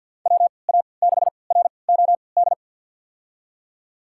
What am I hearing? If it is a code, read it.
Morse code, heard as a keyed tone, about 52 wpm, spaced as for 29 wpm, 690 Hz. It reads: WA6ROD
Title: USA Call Signs - 6 Characters - 3x Character Spacing - 50wpm